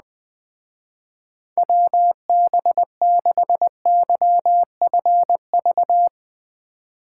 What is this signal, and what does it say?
Morse code, heard as a keyed tone, about 20 wpm, 700 Hz. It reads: WB6YFV
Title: Speed Racing - USA 2x3 Call Signs 17wpm